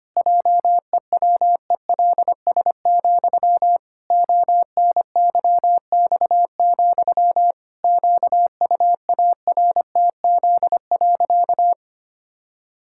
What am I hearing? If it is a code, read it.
Morse code, heard as a keyed tone, about 25 wpm, 700 Hz. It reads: JEWELS, ONYX, QUARTZ.
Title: Pangram Challenge v2 - 25wpm